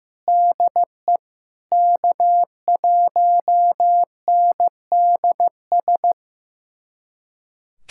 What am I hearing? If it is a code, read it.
Morse code, heard as a keyed tone, about 16 wpm, 700 Hz. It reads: DE K1NDS
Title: USA Callsign DE Callsign 15wpm